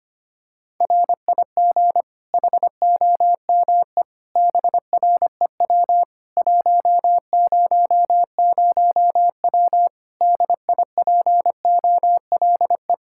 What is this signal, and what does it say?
Morse code, heard as a keyed tone, about 25 wpm, 700 Hz. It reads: RIG HOME BREW 100W DIPOLE
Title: Practice QSO - Exchange 2 - Info - 25wpm